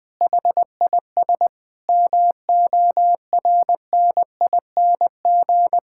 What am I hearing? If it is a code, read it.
Morse code, heard as a keyed tone, about 21 wpm, 700 Hz. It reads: HIS MORNING
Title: Sentences from Top 1000 Words - Review Part 5 - 20wpm